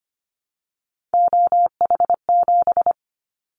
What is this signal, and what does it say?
Morse code, heard as a keyed tone, about 25 wpm, 700 Hz. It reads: OH7
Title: Callsign Prefixes for Top 50 DXCC - 25wpm